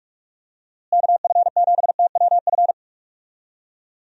Morse code, heard as a keyed tone, about 46 wpm, 700 Hz. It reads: KF7TWF
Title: USA Call Signs - 6 Characters - Repeated 3x - 45wpm